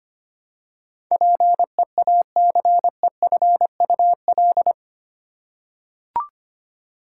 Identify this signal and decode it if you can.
Morse code, heard as a keyed tone, about 25 wpm, 700 Hz. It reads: PEACEFUL
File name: Eight Letter Words - 25wpm